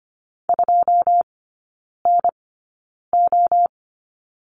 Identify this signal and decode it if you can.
Morse code, heard as a keyed tone, about 25 wpm, spaced as for 9 wpm, 690 Hz. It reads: 2NO
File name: Lesson 14e - Number 2 - Three Characters 25wpm